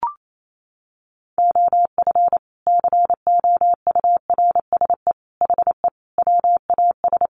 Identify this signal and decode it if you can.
Morse code, heard as a keyed tone, about 28 wpm, 700 Hz. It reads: OF COURSE HE WAS
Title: Sentences from Top 500 Words 28wpm